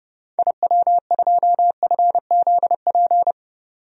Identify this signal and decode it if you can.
Morse code, heard as a keyed tone, about 30 wpm, 700 Hz. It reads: IW2FZP
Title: DXCC-100 Call Signs - 6 Characters - 30wpm